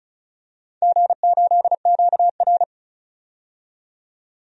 Morse code, heard as a keyed tone, about 34 wpm, 700 Hz. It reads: G8QR
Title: DXCC-100 Call Signs - 4 Characters - 35wpm